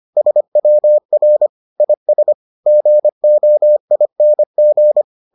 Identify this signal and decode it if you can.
Morse code, heard as a keyed tone, about 25 wpm, 600 Hz. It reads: SWR IS GOING